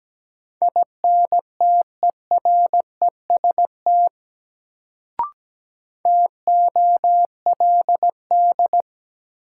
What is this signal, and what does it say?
Morse code, heard as a keyed tone, about 17 wpm, 700 Hz. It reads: INTEREST TOLD
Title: Sentences from Top 400 Words - New Word Drill - 17wpm